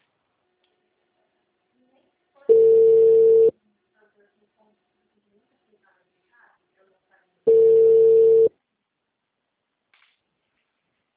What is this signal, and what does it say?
Ringback tone (China, 450 Hz), with speech.
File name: out- -2014- - - .56